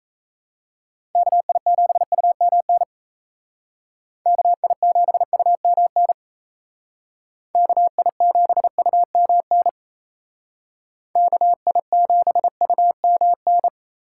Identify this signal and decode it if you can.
Morse code, heard as a keyed tone, about 34 wpm, 700 Hz. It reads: KI7UMN KI7UMN KI7UMN KI7UMN
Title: Speed Racing - USA 2x3 Call Signs 28wpm